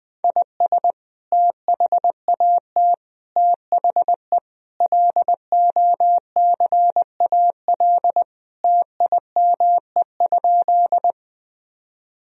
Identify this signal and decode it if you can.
Morse code, heard as a keyed tone, about 21 wpm, 700 Hz. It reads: IS THAT THE LOCAL TIME?
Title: Sentences from Top 1000 Words - Review Part 3 - 20wpm